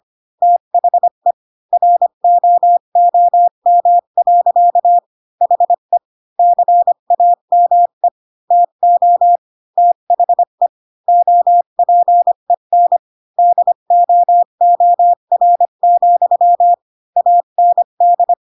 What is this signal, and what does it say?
Morse code, heard as a keyed tone, about 25 wpm, 700 Hz. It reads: THE ROOM. HE CAME TO THE OPEN DOOR, AND
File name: Pinocchio / Pinocchio ch3 20 25 ik1hge